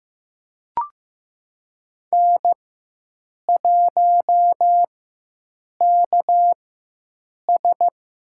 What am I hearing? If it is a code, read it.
Morse code, heard as a keyed tone, about 16 wpm, spaced as for 7 wpm, 700 Hz. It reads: N1KS
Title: USA Call Signs - 4 Characters - 4x Character Spacing - 15wpm